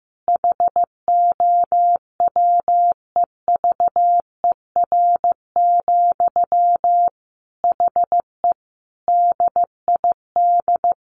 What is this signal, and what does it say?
Morse code, heard as a keyed tone, about 15 wpm, 700 Hz. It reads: HOWEVER, HE DID
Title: Sentences from Top 600 Words 15wpm